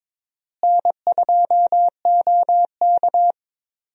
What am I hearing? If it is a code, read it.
Morse code, heard as a keyed tone, about 22 wpm, 700 Hz. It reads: N2OK
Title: USA Call Signs - 4 Characters - Repeated 3x - 22wpm